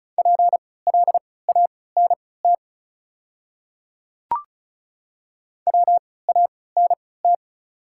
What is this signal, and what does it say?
Morse code, heard as a keyed tone, about 36 wpm, spaced as for 20 wpm, 690 Hz. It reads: PLANT WANT
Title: 3x Character Spacing - Top 500 Words - 35wpm